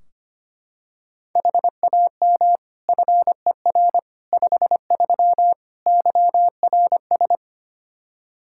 Morse code, heard as a keyed tone, about 25 wpm, 700 Hz. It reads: HAM FER 53 YRS
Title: Practice QSO - Exchange 3 - More Info - 25wpm